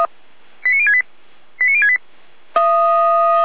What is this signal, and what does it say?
Five-tone selective call (ZVEI), heard as quick runs of five tones, tones of about 70 ms. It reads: 89068 89068